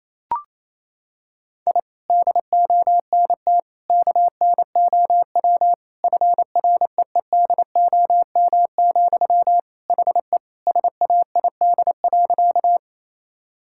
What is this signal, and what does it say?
Morse code, heard as a keyed tone, about 28 wpm, 700 Hz. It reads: I DONT KNOW FREEDOM, HE SAID.
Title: Sentences from Top 1000 Words - Review Part 5 - 28wpm